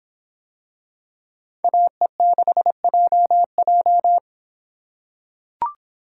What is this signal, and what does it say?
Morse code, heard as a keyed tone, about 24 wpm, 700 Hz. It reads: AE6JJ